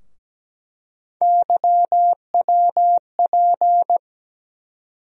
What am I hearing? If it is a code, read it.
Morse code, heard as a keyed tone, about 17 wpm, 700 Hz. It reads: YWP
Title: USA Call Sign Postfixes - 3 Letters 17wpm